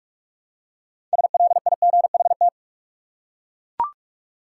Morse code, heard as a keyed tone, about 41 wpm, 700 Hz. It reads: SLIGHT
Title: Six Letter Words - 45wpm